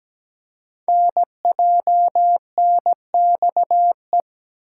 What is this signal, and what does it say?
Morse code, heard as a keyed tone, about 17 wpm, 700 Hz. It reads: NJNXE
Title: Code Groups - 5 Letters - 17wpm